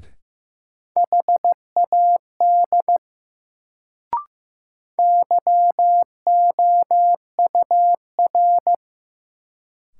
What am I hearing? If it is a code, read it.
Morse code, heard as a keyed tone, about 15 wpm, 700 Hz. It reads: HAD YOUR
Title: Sets of 2 Words - New Word Drill - 15wpm